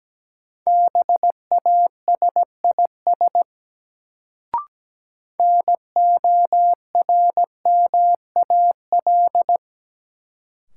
Morse code, heard as a keyed tone, about 17 wpm, 700 Hz. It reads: BASIS NORMAL
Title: Sentences from Top 1200 Words - New Word Drill Alt - 17wpm